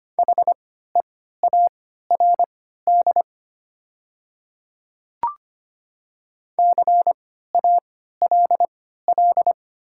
Morse code, heard as a keyed tone, about 25 wpm, spaced as for 14 wpm, 700 Hz. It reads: HEARD CALL
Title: 3x Character Spacing - Top 500 Words - 25wpm